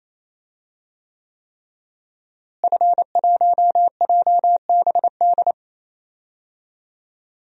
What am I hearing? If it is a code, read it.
Morse code, heard as a keyed tone, about 27 wpm, 700 Hz. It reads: F1JBD
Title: DXCC-100 Call Signs - 5 Characters - Repeated 3x - 28wpm